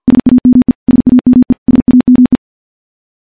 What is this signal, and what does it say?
Morse code, heard as a keyed tone, about 46 wpm, 250 Hz. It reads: SAME SAME SAME
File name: morse / morse-training-50WPM